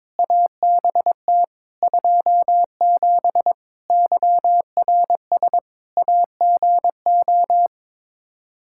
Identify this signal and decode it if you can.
Morse code, heard as a keyed tone, about 22 wpm, 700 Hz. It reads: ABT 27 YRS AGO
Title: Practice QSO - Exchange 3 - More Info - 22wpm